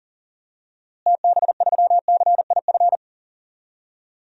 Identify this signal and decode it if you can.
Morse code, heard as a keyed tone, about 40 wpm, 700 Hz. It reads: TB3CIF
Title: DXCC-100 Call Signs - 6 Characters - Repeated 2x - 40wpm